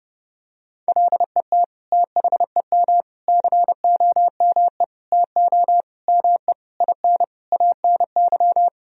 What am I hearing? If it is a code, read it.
Morse code, heard as a keyed tone, about 30 wpm, 700 Hz. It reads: LET THEM COME TO ME IN ANY
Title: Sentences from Top 1000 Words - Review Part 3 - 30wpm